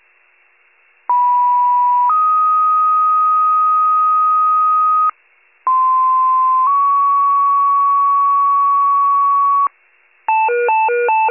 Two-tone page, heard as a one-second tone, then a three-second tone, about 980 then 1250 Hz, about 1040 then 1120 Hz. It is heard from 1.1 to 9.7 s.
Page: 91 FIRE